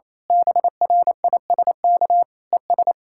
Morse code, heard as a keyed tone, about 26 wpm, 700 Hz. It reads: BRISK ES
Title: Speed-Racing Common QSO Phrases - 28wpm